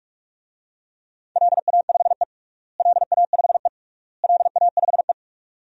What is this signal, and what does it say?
Morse code, heard as a keyed tone, about 46 wpm, 700 Hz. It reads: LA5E LA5E LA5E
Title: DXCC-100 Call Signs - 4 Characters - Repeated 3x - 45wpm